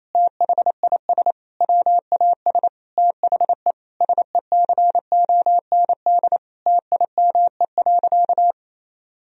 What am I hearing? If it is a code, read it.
Morse code, heard as a keyed tone, about 28 wpm, 700 Hz. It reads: THIS WAS THE SECOND TIME.
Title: Sentences from Top 1000 Words - Review Part 1 - 28wpm